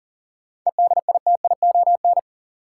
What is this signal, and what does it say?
Morse code, heard as a keyed tone, about 40 wpm, 700 Hz. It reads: EDITION
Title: Sets of 5 Words - New Word Drill - 40wpm